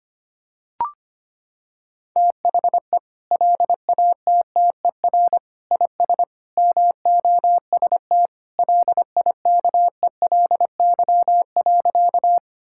Morse code, heard as a keyed tone, about 25 wpm, 700 Hz. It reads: THE LATTER IS MOST LIKELY.
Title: Sentences from Top 1600 Words 25wpm